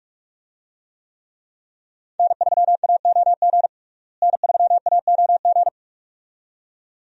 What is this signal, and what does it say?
Morse code, heard as a keyed tone, about 46 wpm, 700 Hz. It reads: N3AOG N3AOG
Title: USA Call Signs - 5 Characters - Repeated 2x - 45wpm